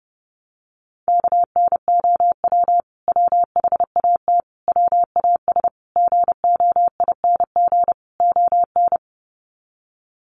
Morse code, heard as a keyed tone, about 30 wpm, 700 Hz. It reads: KNOW WHAT WAS GOING ON
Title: Sets of 5 Words Encore2 30wpm